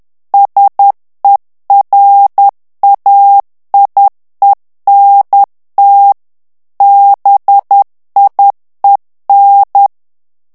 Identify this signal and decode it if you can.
Morse code, heard as a keyed tone, about 10 wpm, 800 Hz. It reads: SERAIENT BIEN
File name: morse mp3 / FRANCAIS TEXTE CLAIR 2 VITESSE 10 MOTS MINUTES